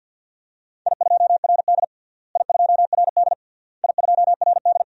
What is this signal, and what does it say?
Morse code, heard as a keyed tone, about 48 wpm, 700 Hz. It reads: I2RD I2RD I2RD
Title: DXCC-100 Call Signs - 4 Characters - Repeated 3x - 50wpm